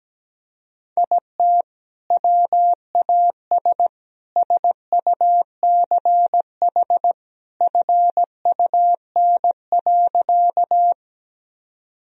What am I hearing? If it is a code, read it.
Morse code, heard as a keyed tone, about 17 wpm, 700 Hz. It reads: IT WAS SUCH FUN.
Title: Sentences from Top 1000 Words - Review Part 4 - 17wpm